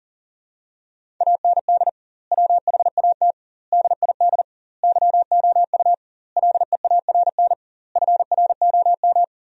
Morse code, heard as a keyed tone, about 40 wpm, 700 Hz. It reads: AND WHAT DID YOU LEARN FROM
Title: 2x Word-Spacing - Sentences from Top 300 Words - 40wpm